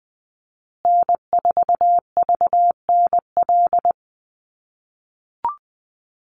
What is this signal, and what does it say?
Morse code, heard as a keyed tone, about 20 wpm, 700 Hz. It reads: N4VNL
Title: USA Call Signs Repeated 3x 20wpm